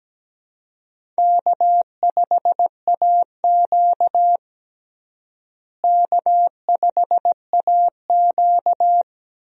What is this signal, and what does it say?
Morse code, heard as a keyed tone, about 17 wpm, 700 Hz. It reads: K5AQ K5AQ
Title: USA Call Signs - 4 Characters - Repeated 2x - 17wpm